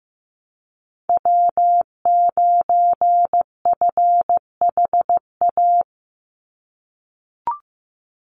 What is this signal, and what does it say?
Morse code, heard as a keyed tone, about 15 wpm, 700 Hz. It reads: W9FHA
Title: NA SOTA Chaser Callsigns 15wpm